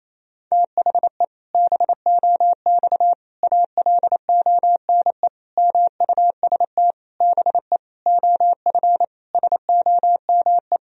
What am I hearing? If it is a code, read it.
Morse code, heard as a keyed tone, about 28 wpm, 700 Hz. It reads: THE BOX ALONE MUST BE OF SOME
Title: Sentences from Top 1000 Words - Review Part 3 - 28wpm